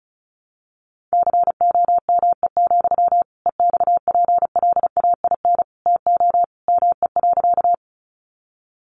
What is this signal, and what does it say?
Morse code, heard as a keyed tone, about 34 wpm, 700 Hz. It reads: COME, EXPLAIN TO ME.
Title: Sentences from Top 1100 Words 35wpm